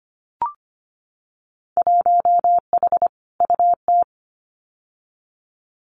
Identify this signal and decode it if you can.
Morse code, heard as a keyed tone, about 25 wpm, 700 Hz. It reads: 1H UT
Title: Winter Field Day Exchanges 25wpm